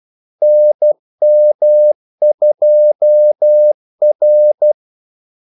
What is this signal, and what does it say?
Morse code, heard as a keyed tone, about 12 wpm, 600 Hz. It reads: NM2R
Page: 2 ekstra ordmellemrum mellem kaldesignalerne